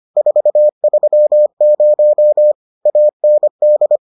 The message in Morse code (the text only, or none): 430 AND